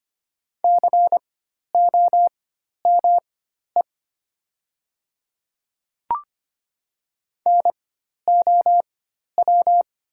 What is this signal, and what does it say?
Morse code, heard as a keyed tone, about 25 wpm, spaced as for 12 wpm, 700 Hz. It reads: COME NOW
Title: 4x Character Spacing - Top 100 Words - 25wpm